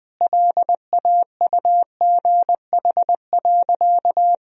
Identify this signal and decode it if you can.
Morse code, heard as a keyed tone, about 20 wpm, 700 Hz. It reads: LAUGH.
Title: Sentences from Top 1000 Words - Review Part 1 - 20wpm